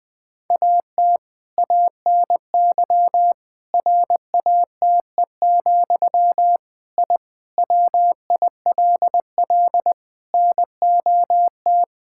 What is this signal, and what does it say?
Morse code, heard as a keyed tone, about 19 wpm, 700 Hz. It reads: AT ANY RATE, I WILL NOT
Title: Sentences from Top 1000 Words - Review Part 6 - 20wpm